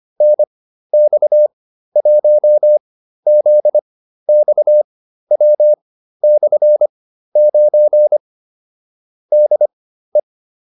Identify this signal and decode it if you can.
Morse code, heard as a keyed tone, about 25 wpm, spaced as for 13 wpm, 600 Hz. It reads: NX1ZXW/9 DE
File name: QSO307 13